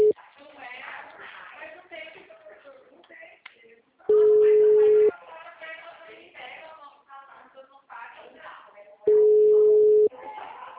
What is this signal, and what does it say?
Ringback tone (Europe, 425 Hz), with speech.
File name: out- -2020- - - .68566